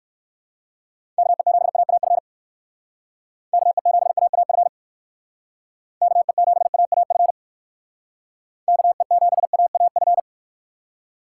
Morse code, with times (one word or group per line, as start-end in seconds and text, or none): 1.2-2.2 KE7AAF
3.5-4.7 KE7AAF
6.0-7.3 KE7AAF
8.7-10.2 KE7AAF